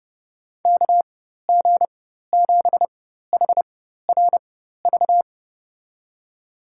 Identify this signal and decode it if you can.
Morse code, heard as a keyed tone, about 32 wpm, spaced as for 14 wpm, 700 Hz. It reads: KG7HRV
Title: USA Call Signs - 6 Characters - 4x Character Spacing - 30wpm